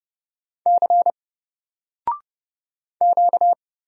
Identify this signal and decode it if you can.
Morse code, heard as a keyed tone, about 30 wpm, 700 Hz. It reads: C Q
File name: Single-Letter-Number-30wpm